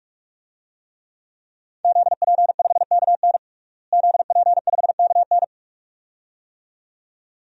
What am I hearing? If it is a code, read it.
Morse code, heard as a keyed tone, about 41 wpm, 700 Hz. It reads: ZP5KN ZP5KN
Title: DXCC-100 Call Signs - 5 Characters - Repeated 2x - 45wpm